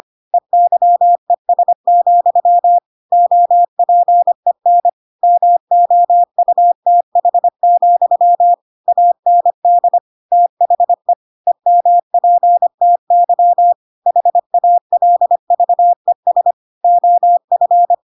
Morse code, heard as a keyed tone, about 25 wpm, 700 Hz. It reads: EYES, OPEN MOUTH, AND THE EMPTY HALVES OF
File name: Pinocchio / Pinocchio ch5 25 25 ik1hge